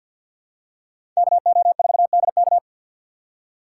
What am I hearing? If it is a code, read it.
Morse code, heard as a keyed tone, about 46 wpm, 700 Hz. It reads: KO4DK
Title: USA Call Signs - 5 Characters - 6x Character Spacing - 50wpm